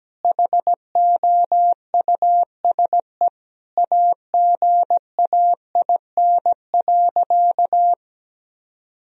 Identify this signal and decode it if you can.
Morse code, heard as a keyed tone, about 17 wpm, 700 Hz. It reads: HOUSE AGAIN.
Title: Sentences from Top 1000 Words - Review Part 1 - 17wpm